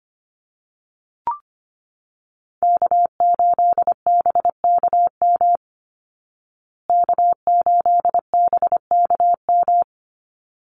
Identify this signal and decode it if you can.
Morse code, heard as a keyed tone, about 25 wpm, 700 Hz. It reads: K8BKM K8BKM
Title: Call Signs Repeated 3x - 25wpm